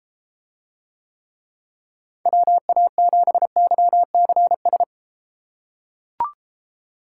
Morse code, heard as a keyed tone, about 32 wpm, 700 Hz. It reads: WA7YCS